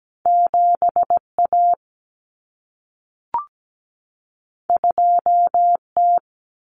Morse code, heard as a keyed tone, about 17 wpm, 700 Hz. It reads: 7A 2T
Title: Common Cut Numbers - 2-Digit 17wpm